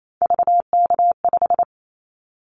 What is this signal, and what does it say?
Morse code, heard as a keyed tone, about 28 wpm, 700 Hz. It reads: VK5
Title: Callsign Prefixes for Top 50 DXCC - 28wpm